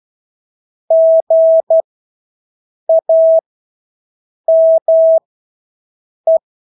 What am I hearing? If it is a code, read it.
Morse code, heard as a keyed tone, about 12 wpm, spaced as for 6 wpm, 650 Hz. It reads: GAME
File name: Engelske ord 12-6 wpm